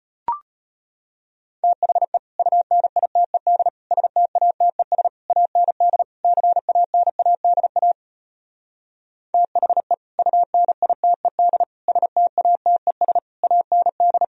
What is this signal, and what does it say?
Morse code, heard as a keyed tone, about 33 wpm, 700 Hz. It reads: THE UNITED STATES AND CANADA THE UNITED STATES AND